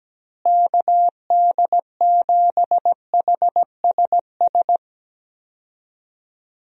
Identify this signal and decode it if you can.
Morse code, heard as a keyed tone, about 17 wpm, 700 Hz. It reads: KD7HSS
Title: USA Call Signs - 6 Characters - Repeated 2x - 17wpm